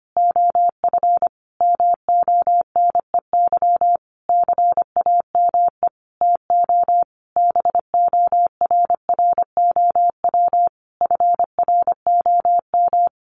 Morse code, heard as a keyed tone, about 25 wpm, 700 Hz. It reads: OF MONEY CAME TO BORROW FROM
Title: Sentences from Top 1500 Words 25wpm